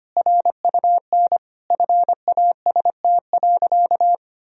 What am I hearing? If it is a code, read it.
Morse code, heard as a keyed tone, about 25 wpm, 700 Hz. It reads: RUN FAST.
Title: Sentences from Top 1000 Words - Review Part 2 - 25wpm